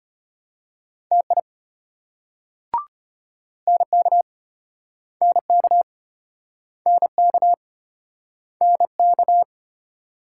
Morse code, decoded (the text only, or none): TI NK NK NK NK